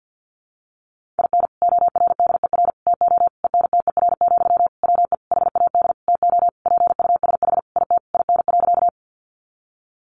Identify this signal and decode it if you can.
Morse code, heard as a keyed tone, about 50 wpm, 700 Hz. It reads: IN ORDER TO ENTER, WE HAD TO PUSH IT IN.
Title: Sentences from Top 1300 Words 50wpm